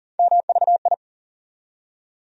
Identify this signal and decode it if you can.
Morse code, heard as a keyed tone, about 40 wpm, 700 Hz. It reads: MVI
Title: USA Call Sign Postfixes - 3 Letters 40wpm